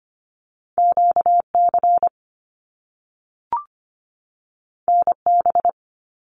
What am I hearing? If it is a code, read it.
Morse code, heard as a keyed tone, about 25 wpm, 700 Hz. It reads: QC NB
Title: Canadian Provinces and Territories 25wpm